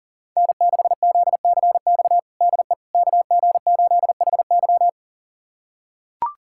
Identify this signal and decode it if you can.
Morse code, heard as a keyed tone, about 40 wpm, 700 Hz. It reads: N6ZCX DE KG8HY
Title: USA Callsign DE Callsign 40wpm